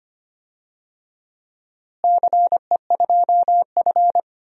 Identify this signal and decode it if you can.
Morse code, heard as a keyed tone, about 25 wpm, 700 Hz. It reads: CE2F
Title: DXCC-100 Call Signs - 4 Characters - Repeated 3x - 25wpm